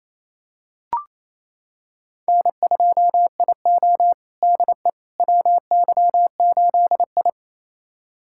Morse code, heard as a keyed tone, about 28 wpm, 700 Hz. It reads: N2IO DE WY8I
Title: USA Callsign DE Callsign - 4 Characters - 28wpm